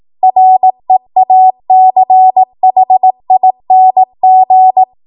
Morse code, heard as a keyed tone, about 18 wpm, 750 Hz. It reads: REACHING
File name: Morse / 18WPM